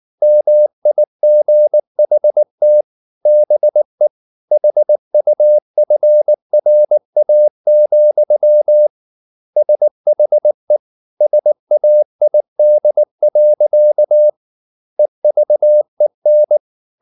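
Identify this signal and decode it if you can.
Morse code, heard as a keyed tone, about 19 wpm, 600 Hz. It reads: MIGHT BE HUFRA, SHE SAID. EVEN
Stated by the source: Morse code at 19 wpm from CW med Gnister, reading Ken Follett's novel Never in English.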